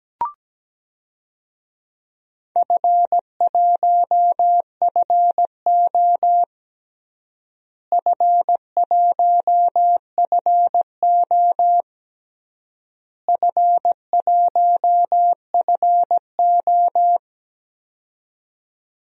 Morse code, heard as a keyed tone, about 17 wpm, 700 Hz. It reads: F1FO F1FO F1FO
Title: DXCC-100 Call Signs - 4 Characters - Repeated 3x - 17wpm